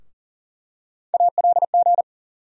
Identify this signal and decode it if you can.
Morse code, heard as a keyed tone, about 40 wpm, 700 Hz. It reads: ALG
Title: USA Call Sign Postfixes - 3 Letters 40wpm